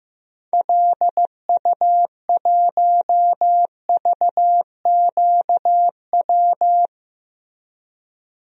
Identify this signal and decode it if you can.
Morse code, heard as a keyed tone, about 15 wpm, 700 Hz. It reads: LU1VQW
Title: DXCC-100 Call Signs - 6 Characters - Repeated 2x - 15wpm